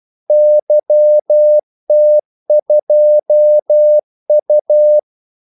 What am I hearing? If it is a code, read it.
Morse code, heard as a keyed tone, about 12 wpm, 600 Hz. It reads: YT2U
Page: 2 ekstra ordmellemrum mellem kaldesignalerne